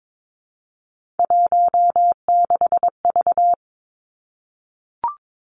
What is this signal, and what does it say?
Morse code, heard as a keyed tone, about 22 wpm, 700 Hz. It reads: 16V
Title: Three Challenging Letter-Number Combinations 22wpm